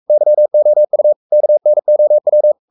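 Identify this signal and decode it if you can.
Morse code, heard as a keyed tone, about 44 wpm, 600 Hz. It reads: YOU KNOW
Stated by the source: Morse code at 43 wpm sending Ken Follett's novel Never in English, from CW med Gnister.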